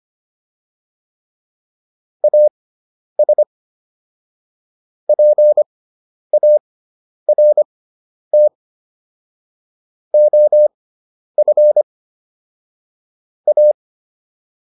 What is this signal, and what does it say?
Morse code, heard as a keyed tone, about 25 wpm, spaced as for 10 wpm, 600 Hz. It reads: AS PART OF A